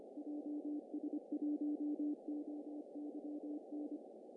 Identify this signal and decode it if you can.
Morse code, heard as a keyed tone, about 25 wpm, 300 Hz. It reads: JS1OYN
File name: CW / cqww-cw-k5tr